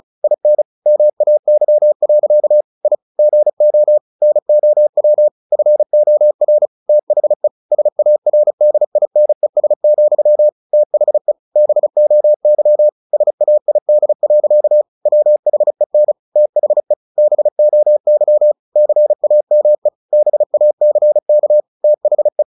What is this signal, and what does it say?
Morse code, heard as a keyed tone, about 34 wpm, 600 Hz. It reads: IN MAY. I GO NOW FOR THE SARDINES, THE BOY SAID. WHEN THE BOY CAME BACK THE
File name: the old man and the sea / the old man and the sea 1 35wpm